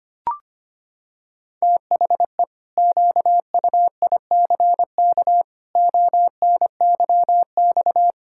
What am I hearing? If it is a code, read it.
Morse code, heard as a keyed tone, about 25 wpm, 700 Hz. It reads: THE QUICK ONYX
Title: Pangram Challenge v2 - 25wpm